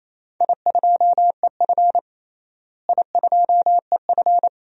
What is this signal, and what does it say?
Morse code, heard as a keyed tone, about 28 wpm, 700 Hz. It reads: I2EF I2EF
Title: DXCC-100 Call Signs - 4 Characters - Repeated 3x - 28wpm